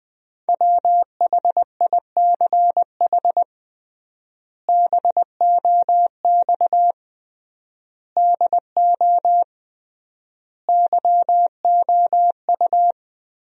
Morse code, heard as a keyed tone, about 20 wpm, 700 Hz. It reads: WHICH BOX DO YOU
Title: 3x Word-Spacing - Sentences from Top 500 Words - 20wpm